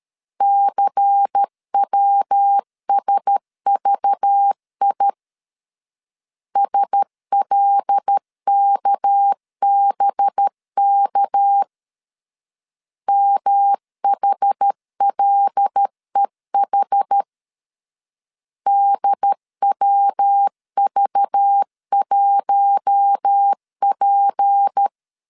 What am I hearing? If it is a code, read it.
Morse code, heard as a keyed tone, about 13 wpm, 800 Hz. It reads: CWSVI SLKBK MHLEH DWV1P
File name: LEZIONI / 6 PKX0 / 6 PK 12wpm 2006